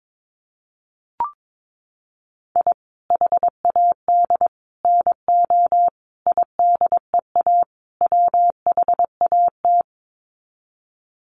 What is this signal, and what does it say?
Morse code, heard as a keyed tone, about 22 wpm, 700 Hz. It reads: I HAD NO IDEA WHAT
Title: Sets of 5 Words Encore 22wpm